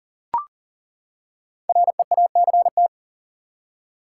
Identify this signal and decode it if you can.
Morse code, heard as a keyed tone, about 40 wpm, 700 Hz. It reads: REACT
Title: Five Letter Words - 40wpm